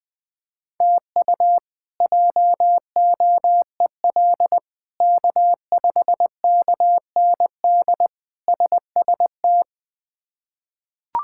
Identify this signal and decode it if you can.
Morse code, heard as a keyed tone, about 21 wpm, 700 Hz. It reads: TU JOEL K5KND SST
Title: K1USN SST QSO Practice - 20wpm